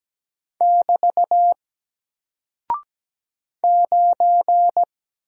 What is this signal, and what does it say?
Morse code, heard as a keyed tone, about 17 wpm, 700 Hz. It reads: <BT> 9
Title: Single Character (Letter:Number:Punctuation:Prosign) 17wpm